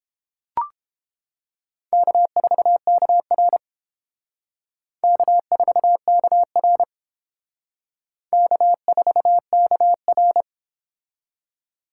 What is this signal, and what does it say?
Morse code, heard as a keyed tone, about 30 wpm, 700 Hz. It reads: K4KR K4KR K4KR